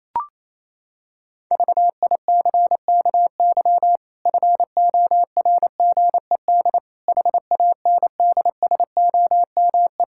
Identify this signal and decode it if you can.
Morse code, heard as a keyed tone, about 27 wpm, 700 Hz. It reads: VICKY FORGED HANDSOME
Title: Pangram Challenge v2 - 28wpm